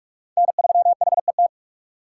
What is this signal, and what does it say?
Morse code, heard as a keyed tone, about 43 wpm, 700 Hz. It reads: N3HET
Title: USA Call Signs - 5 Characters - 4x Character Spacing - 45wpm